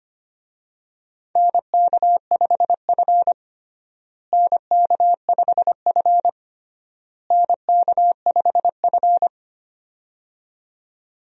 Morse code, heard as a keyed tone, about 25 wpm, 700 Hz. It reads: NK5F NK5F NK5F
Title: USA Call Signs - 4 Characters - Repeated 3x - 25wpm